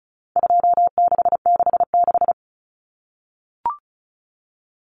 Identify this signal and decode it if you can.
Morse code, heard as a keyed tone, about 34 wpm, 700 Hz. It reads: 2666
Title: Common Cut Numbers - 4-Digit 35wpm